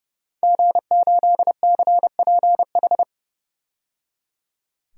Morse code, heard as a keyed tone, about 31 wpm, 700 Hz. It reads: G8CPH
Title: DXCC-100 Call Signs - 5 Characters - 30wpm